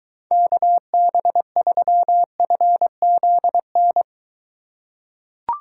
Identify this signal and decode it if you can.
Morse code, heard as a keyed tone, about 23 wpm, 700 Hz. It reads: KB3FZN